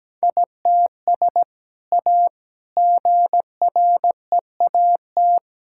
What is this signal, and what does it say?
Morse code, heard as a keyed tone, about 17 wpm, 700 Hz. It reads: ITS A GREAT
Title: Sentences from Top 1000 Words - Review Part 6 - 17wpm